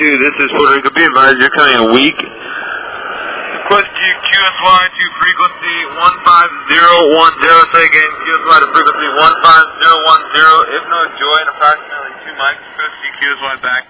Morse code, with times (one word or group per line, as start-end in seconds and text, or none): none